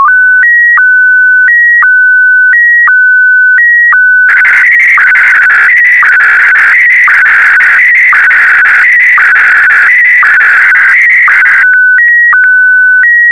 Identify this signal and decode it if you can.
SSTV (slow-scan television), Scottie DX.